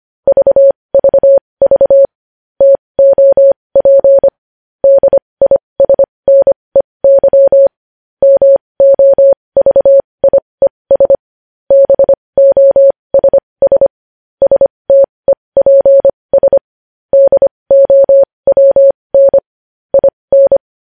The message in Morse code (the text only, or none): VVV TOP DISNEY MOVIES BOSS STEPS DOWN IN